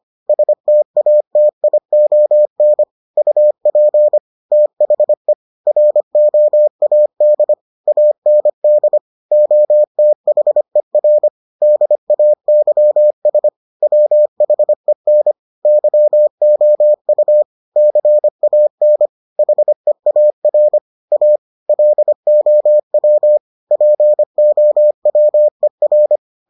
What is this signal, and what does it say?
Morse code, heard as a keyed tone, about 25 wpm, 600 Hz. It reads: STATION UP THE ROAD AND OTHER DAYS WHEN YOU CAN HEAR A LOW POWER